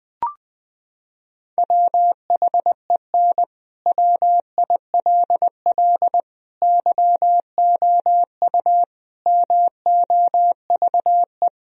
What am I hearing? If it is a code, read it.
Morse code, heard as a keyed tone, about 20 wpm, 700 Hz. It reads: WHEN WILL YOU MOVE
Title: Sentences from Top 1000 Words - Review Part 1 - 20wpm